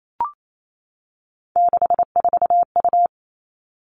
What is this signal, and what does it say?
Morse code, heard as a keyed tone, about 28 wpm, 700 Hz. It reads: 64U
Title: Three Challenging Letter-Number Combinations 28wpm